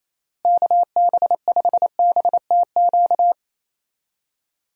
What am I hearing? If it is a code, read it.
Morse code, heard as a keyed tone, about 28 wpm, 700 Hz. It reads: KB5BTQ
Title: USA Call Signs - 6 Characters - Repeated 3x - 28wpm